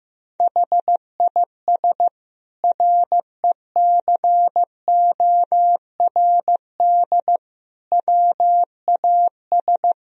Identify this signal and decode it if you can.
Morse code, heard as a keyed tone, about 16 wpm, 700 Hz. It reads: HIS RECORD WAS
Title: Sentences from Top 1000 Words - Review Part 3 - 15wpm